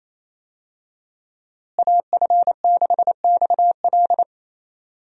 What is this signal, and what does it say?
Morse code, heard as a keyed tone, about 27 wpm, 700 Hz. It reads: AF6XL
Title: USA Call Signs - 5 Characters - Repeated 3x - 28wpm